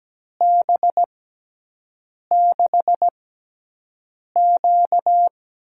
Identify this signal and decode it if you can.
Morse code, heard as a keyed tone, about 17 wpm, spaced as for 6 wpm, 700 Hz. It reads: B6Q
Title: USA Call Signs - 4 Characters - 6x Character Spacing - 17wpm